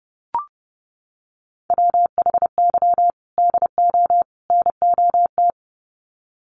Morse code, heard as a keyed tone, about 30 wpm, 700 Hz. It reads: WHY DO NOT
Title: Sets of 3 Words 30wpm